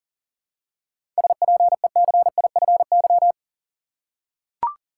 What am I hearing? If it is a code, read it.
Morse code, heard as a keyed tone, about 40 wpm, 700 Hz. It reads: SPECIFY
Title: Seven Letter Words - 40wpm